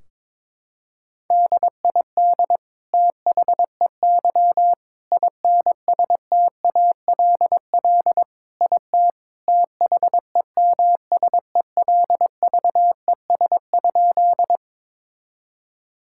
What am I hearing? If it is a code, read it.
Morse code, heard as a keyed tone, about 22 wpm, 700 Hz. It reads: DID THEY INSTALL IT THEMSELVES?
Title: Sentences from Top 1600 Words 22wpm